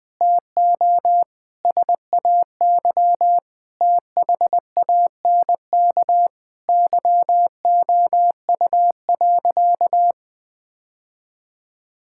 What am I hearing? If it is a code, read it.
Morse code, heard as a keyed tone, about 20 wpm, 700 Hz. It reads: TO SAY THANK YOU.
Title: Sentences with 2-Word Phrases - Spoken Beginning - 20wpm